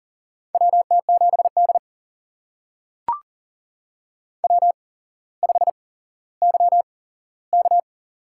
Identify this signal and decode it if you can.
Morse code, heard as a keyed tone, about 39 wpm, spaced as for 17 wpm, 700 Hz. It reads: WT7D W5YK
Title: USA Call Signs - 4 Characters - 8x Character Spacing - 40wpm